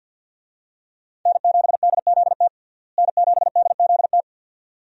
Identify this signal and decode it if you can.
Morse code, heard as a keyed tone, about 50 wpm, 700 Hz. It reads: N7DZT N7DZT
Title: USA Call Signs - 5 Characters - Repeated 2x - 50wpm